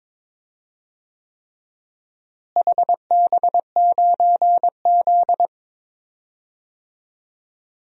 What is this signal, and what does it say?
Morse code, heard as a keyed tone, about 22 wpm, 700 Hz. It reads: HB9Z
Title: DXCC-100 Call Signs - 4 Characters - Repeated 3x - 22wpm